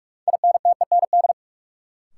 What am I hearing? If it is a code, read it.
Morse code, heard as a keyed tone, about 47 wpm, 700 Hz. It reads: INTEND
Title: Sentences from Top 1600 Words - New Word Drill - 45wpm